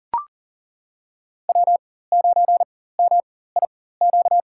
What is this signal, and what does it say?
Morse code, heard as a keyed tone, about 40 wpm, spaced as for 19 wpm, 700 Hz. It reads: W9MIQ
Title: USA Call Signs - 5 Characters - 4x Character Spacing - 40wpm